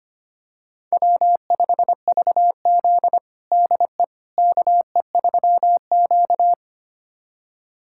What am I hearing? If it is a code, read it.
Morse code, heard as a keyed tone, about 25 wpm, 700 Hz. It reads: W5VZ DE KE3Q
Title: USA Callsign DE Callsign - 4 Characters - 25wpm